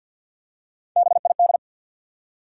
Morse code, heard as a keyed tone, about 51 wpm, 700 Hz. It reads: BID
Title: Three Letter Words - 50wpm